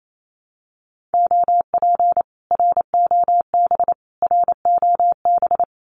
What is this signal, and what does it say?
Morse code, heard as a keyed tone, about 28 wpm, 700 Hz. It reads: OP ROB ROB
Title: Common QSO Phrases Encore 28wpm